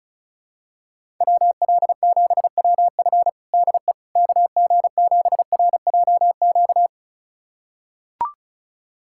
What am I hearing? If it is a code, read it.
Morse code, heard as a keyed tone, about 36 wpm, 690 Hz. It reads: WL7WF DE KG7RJQ
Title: USA Callsign DE Callsign 35wpm